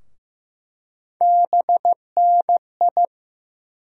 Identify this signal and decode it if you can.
Morse code, heard as a keyed tone, about 15 wpm, 700 Hz. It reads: BNI
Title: USA Call Sign Postfixes - 3 Letters 15wpm